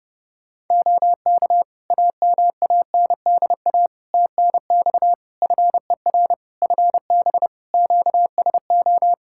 Morse code, heard as a keyed tone, about 30 wpm, 700 Hz. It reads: OK AMANDA TNX FER FB QSO
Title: Practice QSO - Exchange 4 - Ending - 30wpm